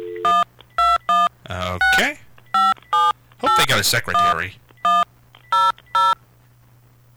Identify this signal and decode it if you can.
DTMF, heard as telephone touch tones, tones of about 210 ms, with speech. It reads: 12165781100